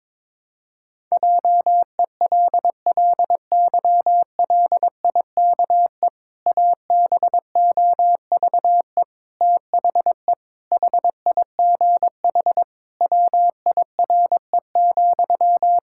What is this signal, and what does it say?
Morse code, heard as a keyed tone, about 22 wpm, 700 Hz. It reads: JELLYLIKE ABOVE THE HIGH WIRE,
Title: Pangram Challenge v2 - 22wpm